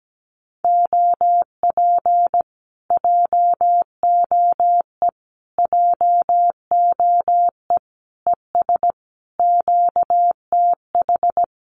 Morse code, heard as a keyed tone, about 17 wpm, 700 Hz. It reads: OP JOE JOE ES QTH
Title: Common QSO Phrases Encore 17wpm